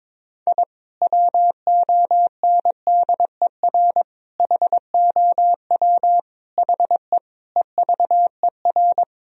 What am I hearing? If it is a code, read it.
Morse code, heard as a keyed tone, about 22 wpm, 700 Hz. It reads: I WONDER HOW HE EVER
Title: Sentences from Top 1000 Words - Review Part 3 - 22wpm